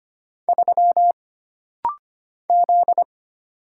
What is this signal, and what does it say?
Morse code, heard as a keyed tone, about 25 wpm, 700 Hz. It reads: 3 Z
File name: Single Character (Letter:Number:Punctuation:Prosign) - Rapid-Fire - 25wpm